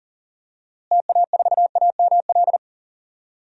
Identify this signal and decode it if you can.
Morse code, heard as a keyed tone, about 40 wpm, 700 Hz. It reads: TA4AML
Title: DXCC-100 Call Signs - 6 Characters - Repeated 3x - 40wpm